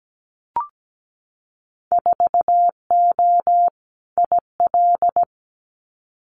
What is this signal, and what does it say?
Morse code, heard as a keyed tone, about 17 wpm, 700 Hz. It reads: 4O IL
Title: Winter Field Day Exchanges Encore 17wpm